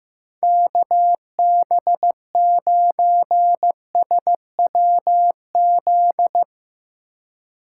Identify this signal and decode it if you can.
Morse code, heard as a keyed tone, about 17 wpm, 700 Hz. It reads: KB9SWZ
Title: Speed Racing - USA Call Signs 15wpm